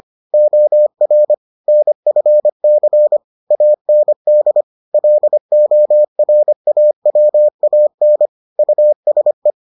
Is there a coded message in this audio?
Yes: Morse code from 0.3 to 9.5 s, heard as a keyed tone, about 25 wpm, 600 Hz. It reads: OR NFC AND LORAWAN USE